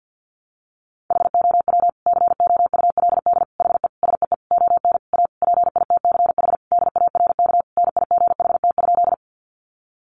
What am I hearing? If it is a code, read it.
Morse code, heard as a keyed tone, about 50 wpm, 700 Hz. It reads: HOW COULD HE SEE ON A PITCH DARK NIGHT?
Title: Sentences from Top 1300 Words 50wpm